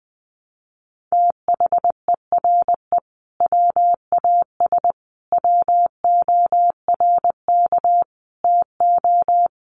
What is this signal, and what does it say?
Morse code, heard as a keyed tone, about 20 wpm, 700 Hz. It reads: THERE WAS WORK TO
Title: Sentences from Top 200 Words 20wpm